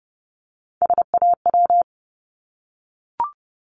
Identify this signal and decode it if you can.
Morse code, heard as a keyed tone, about 30 wpm, 700 Hz. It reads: SAW
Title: Multi-letter Phonemes 30wpm